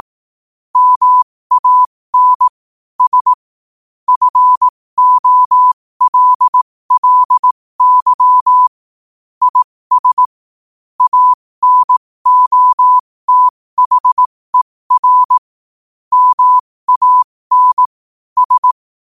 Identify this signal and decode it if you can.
Morse code, heard as a keyed tone, about 18 wpm, spaced as for 15 wpm, 1000 Hz. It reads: MAN S FOLLY IS ANOTHER MAN S
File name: qotdfiles / 15wpm